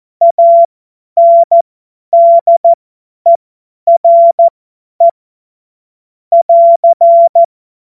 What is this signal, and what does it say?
Morse code, heard as a keyed tone, about 14 wpm, spaced as for 10 wpm, 680 Hz. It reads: ANDERE <AR>
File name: Morsekurs / Witze / Witz 25. Januar 2022, 50 F70